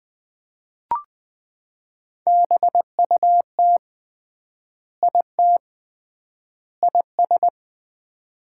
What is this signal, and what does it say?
Morse code, heard as a keyed tone, about 20 wpm, 700 Hz. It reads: BUT IT IS
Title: 3x Word-Spacing - Sets of 3 Words - 20wpm